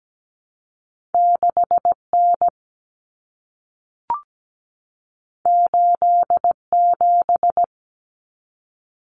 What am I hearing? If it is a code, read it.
Morse code, heard as a keyed tone, about 17 wpm, 700 Hz. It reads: 6N 87
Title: Common Cut Numbers - 2-Digit 17wpm